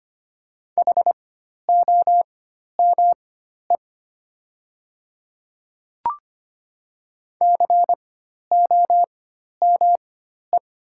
Morse code, heard as a keyed tone, about 25 wpm, spaced as for 12 wpm, 700 Hz. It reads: HOME COME
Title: 4x Character Spacing - Top 200 Words - 25wpm